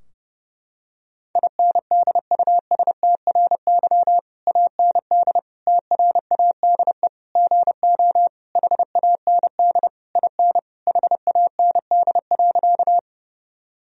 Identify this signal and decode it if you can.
Morse code, heard as a keyed tone, about 30 wpm, 700 Hz. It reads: INDUSTRY AND TRADE GO HAND IN HAND.
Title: Sentences from Top 1000 Words - Review Part 4 - 30wpm